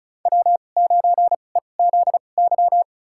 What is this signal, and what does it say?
Morse code, heard as a keyed tone, about 34 wpm, spaced as for 25 wpm, 700 Hz. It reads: W9EZY
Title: USA Call Signs - 5 Characters - 2x Character Spacing - 35wpm